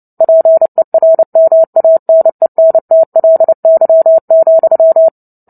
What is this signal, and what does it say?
Morse code, heard as a keyed tone, about 29 wpm, 650 Hz. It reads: PERMANENTLY,
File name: Book Two / 03-Chapter 3 - 29 WPM